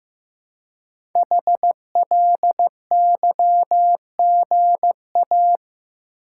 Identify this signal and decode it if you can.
Morse code, heard as a keyed tone, about 15 wpm, 700 Hz. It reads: HLYGA
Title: Code Groups - 5 Letters - 15wpm